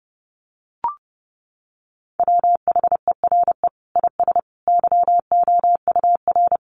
Morse code, heard as a keyed tone, about 30 wpm, 700 Hz. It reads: WHERE IS YOUR
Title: Sentences from Top 1500 Words 30wpm